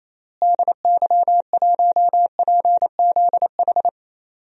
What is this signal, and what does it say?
Morse code, heard as a keyed tone, about 28 wpm, 700 Hz. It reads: DY1PZH
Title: DXCC-100 Call Signs - 6 Characters - Repeated 3x - 28wpm